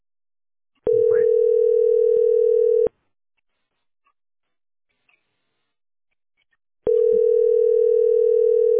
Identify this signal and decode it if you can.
Ringback tone (North America), 440 + 480 Hz.